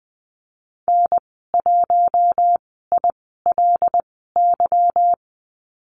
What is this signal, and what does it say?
Morse code, heard as a keyed tone, about 20 wpm, spaced as for 15 wpm, 700 Hz. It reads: N1ILY
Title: 167 Lesson 27g - Number 7 - Call Signs 20wpm